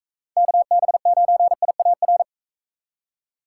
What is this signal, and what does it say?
Morse code, heard as a keyed tone, about 40 wpm, 700 Hz. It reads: KB9IAR